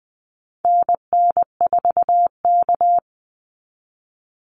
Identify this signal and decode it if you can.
Morse code, heard as a keyed tone, about 20 wpm, 700 Hz. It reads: NN4K
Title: Call Signs Repeated 2x - 20wpm